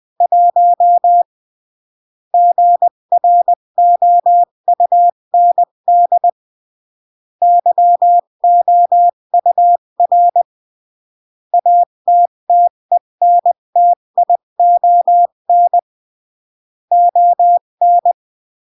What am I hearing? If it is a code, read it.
Morse code, heard as a keyed tone, about 20 wpm, 700 Hz. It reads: 1 GROUND YOUR ATTENTION ON
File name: CW / jodorowsky / 82 commandments 20-15